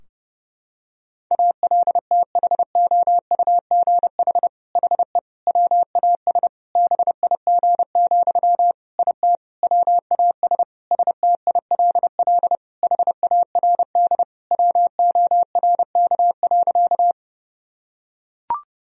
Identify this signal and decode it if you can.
Morse code, heard as a keyed tone, about 29 wpm, 700 Hz. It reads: ALTHOUGH HE WAS BIG, IT WAS STILL HARD WORK.
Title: Sentences from Top 1000 Words - Review Part 3 - 30wpm